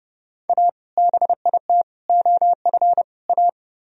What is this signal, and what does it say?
Morse code, heard as a keyed tone, about 35 wpm, 700 Hz. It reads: A BIT OF A
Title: Speed-Racing - Sets of 4 Words 30wpm